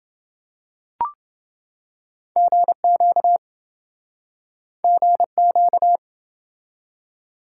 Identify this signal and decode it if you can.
Morse code, heard as a keyed tone, about 27 wpm, 700 Hz. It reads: GQ GQ